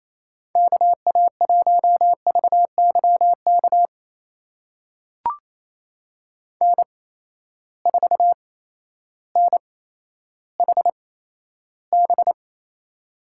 Morse code, heard as a keyed tone, about 28 wpm, spaced as for 9 wpm, 700 Hz. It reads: KA1VY KN4NHB
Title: USA Call Signs - 8x Character Spacing - 28wpm